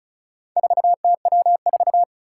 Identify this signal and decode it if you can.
Morse code, heard as a keyed tone, about 36 wpm, 700 Hz. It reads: 4TW4
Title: Four Challenging Letter-Number Combinations 35wpm